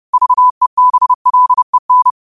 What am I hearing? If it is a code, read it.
Morse code, heard as a keyed tone, about 30 wpm, 1000 Hz. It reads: VEBLEN
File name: qotdfiles / 30wpm